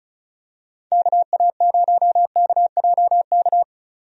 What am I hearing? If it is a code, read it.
Morse code, heard as a keyed tone, about 34 wpm, 700 Hz. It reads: KA0KJK
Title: USA Call Signs - 6 Characters - 35wpm